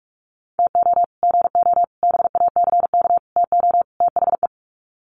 Morse code, heard as a keyed tone, about 44 wpm, 700 Hz. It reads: TO GO BACK TO THE
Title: Sets of 5 Words 45wpm